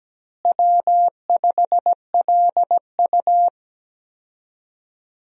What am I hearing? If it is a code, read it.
Morse code, heard as a keyed tone, about 17 wpm, 700 Hz. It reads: W5LU
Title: USA Call Signs - 4 Characters - Repeated 3x - 17wpm